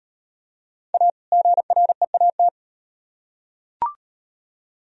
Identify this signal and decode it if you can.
Morse code, heard as a keyed tone, about 36 wpm, 700 Hz. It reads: A GREAT